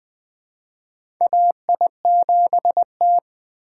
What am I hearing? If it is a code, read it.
Morse code, heard as a keyed tone, about 21 wpm, 700 Hz. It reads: AI7T
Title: USA Call Signs - 2x Character Spacing - 20wpm